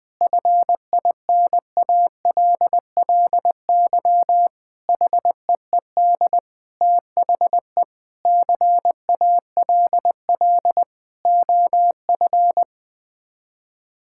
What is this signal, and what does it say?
Morse code, heard as a keyed tone, about 20 wpm, 700 Hz. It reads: FINALLY HEED THE CALL OF
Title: Sets of 8 Words - 20wpm